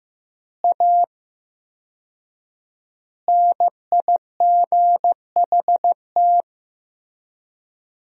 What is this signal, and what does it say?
Morse code, heard as a keyed tone, about 14 wpm, 700 Hz. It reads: A NIGHT
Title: 4x Word-Spacing - Sentences from Top 300 Words - 15wpm